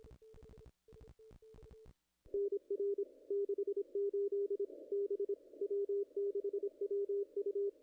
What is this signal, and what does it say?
Morse code, heard as a keyed tone, about 25 wpm, 410 Hz. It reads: NR 68 B W6WU
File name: contests / ss-cw-k5tr-op-n5rz -0100